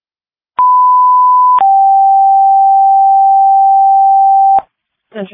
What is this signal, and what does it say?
Two-tone page, heard as a one-second tone, then a three-second tone, about 1000 then 780 Hz. It is heard from 0.6 to 4.6 s.